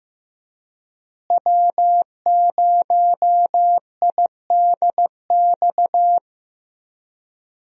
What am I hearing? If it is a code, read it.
Morse code, heard as a keyed tone, about 15 wpm, 700 Hz. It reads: W0IDX
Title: USA Call Signs - 5 Characters - Repeated 3x - 15wpm